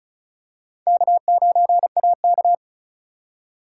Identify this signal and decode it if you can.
Morse code, heard as a keyed tone, about 36 wpm, 700 Hz. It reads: K9AK
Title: USA Call Signs - 4 Characters - 3x Character Spacing - 35wpm